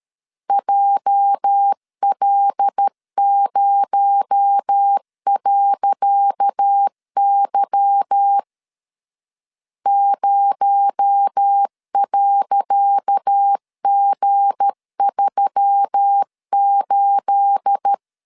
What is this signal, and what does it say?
Morse code, heard as a keyed tone, about 12 wpm, 800 Hz. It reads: JL0.Y 0.G38
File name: LEZIONI / 10 SIGNS / 10C DOT 12wpm 2451